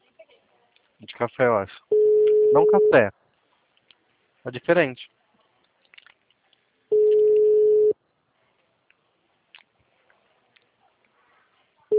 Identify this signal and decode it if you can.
Ringback tone (Europe, 425 Hz), with speech.